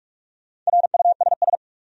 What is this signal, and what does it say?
Morse code, heard as a keyed tone, about 42 wpm, 700 Hz. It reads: RUSS
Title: USA Common Names - 45wpm